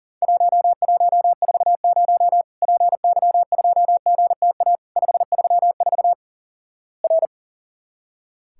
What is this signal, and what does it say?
Morse code, heard as a keyed tone, about 43 wpm, 700 Hz. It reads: 1140 PY2GTA 534 R
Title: Worked All Europe DX Contest - QTC Exchange - 40wpm